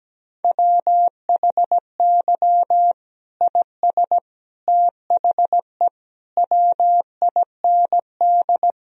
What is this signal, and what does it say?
Morse code, heard as a keyed tone, about 17 wpm, 700 Hz. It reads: WHY IS THE WIND
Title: Sentences from Top 1000 Words - Review Part 2 - 17wpm